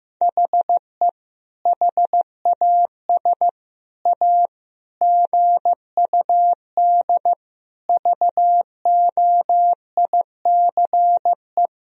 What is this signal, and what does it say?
Morse code, heard as a keyed tone, about 15 wpm, 700 Hz. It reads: HE HAS A GUD VOICE
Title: Sentences from Top 500 Words and Abbreviations 15wpm